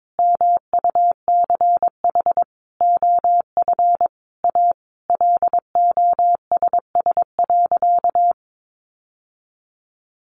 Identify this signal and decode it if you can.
Morse code, heard as a keyed tone, about 22 wpm, 700 Hz. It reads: MUCH OF A LOSS.
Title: Sentences from Top 1400 Words 22wpm